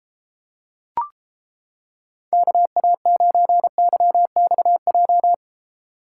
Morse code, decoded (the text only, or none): KA9YXJ